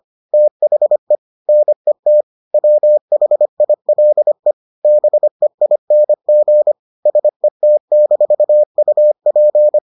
Morse code, heard as a keyed tone, about 25 wpm, 600 Hz. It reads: THE NET WHILE BEING SET-UP